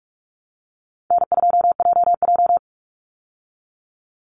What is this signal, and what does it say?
Morse code, heard as a keyed tone, about 44 wpm, 700 Hz. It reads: N2JJ
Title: Call Signs Repeated 3x - 45wpm